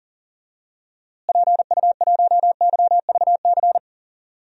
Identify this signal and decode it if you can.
Morse code, heard as a keyed tone, about 38 wpm, 700 Hz. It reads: PU1YVC
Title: DXCC-100 Call Signs - 6 Characters - Repeated 3x - 40wpm